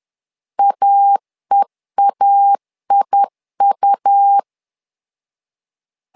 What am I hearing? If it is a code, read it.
Morse code, heard as a keyed tone, about 10 wpm, 800 Hz. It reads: AEAIU
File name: LEZIONI / 2 UHDN / 2 UHD 10wpm 1826